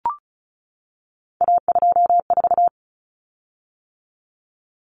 Morse code, heard as a keyed tone, about 34 wpm, 700 Hz. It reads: A24
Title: Common Cut Numbers - 3-Digit 35wpm